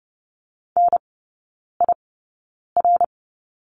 Morse code, heard as a keyed tone, about 30 wpm, spaced as for 9 wpm, 700 Hz. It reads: NIR
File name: Lesson 15e - Number 5 - Three Characters 30wpm